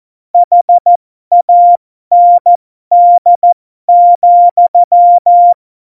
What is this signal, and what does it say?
Morse code, heard as a keyed tone, about 14 wpm, spaced as for 12 wpm, 700 Hz. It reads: HAND,
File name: Morsekurs / Witze